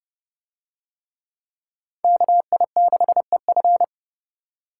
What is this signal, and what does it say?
Morse code, heard as a keyed tone, about 26 wpm, 700 Hz. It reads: KI6EF